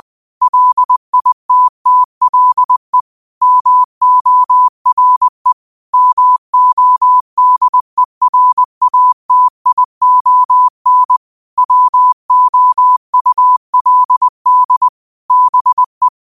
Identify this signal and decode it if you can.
Morse code, heard as a keyed tone, about 20 wpm, 1000 Hz. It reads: LITTLE MORE MODERATION WOULD BE